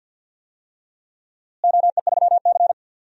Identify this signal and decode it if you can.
Morse code, heard as a keyed tone, about 46 wpm, 700 Hz. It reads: OE3C
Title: DXCC-100 Call Signs - 4 Characters - Repeated 3x - 50wpm